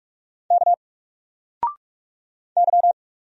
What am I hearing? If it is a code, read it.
Morse code, heard as a keyed tone, about 46 wpm, 700 Hz. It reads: K Y
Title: Single Character (Letter:Number:Punctuation:Prosign) 45wpm